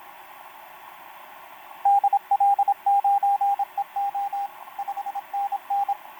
Morse code, heard as a keyed tone, about 27 wpm, 800 Hz, with radio noise. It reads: DL9EO 5NN
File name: pj4-w9njy f5upo etc 12m cw 28mar2011 1924